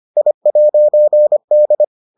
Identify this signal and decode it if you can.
Morse code, heard as a keyed tone, about 25 wpm, 600 Hz. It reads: I'D